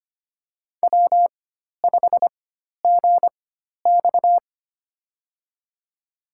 Morse code, heard as a keyed tone, about 25 wpm, spaced as for 12 wpm, 700 Hz. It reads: W5GX
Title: USA Call Signs - 4 Characters - 4x Character Spacing - 25wpm